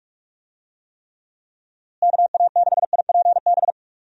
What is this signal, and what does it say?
Morse code, heard as a keyed tone, about 45 wpm, 700 Hz. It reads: KA6IPB
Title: USA Call Signs - 6 Characters - Repeated 2x - 45wpm